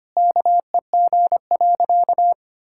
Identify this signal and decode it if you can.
Morse code, heard as a keyed tone, about 25 wpm, 700 Hz. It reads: KEG.
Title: Pangram Challenge v2 - 25wpm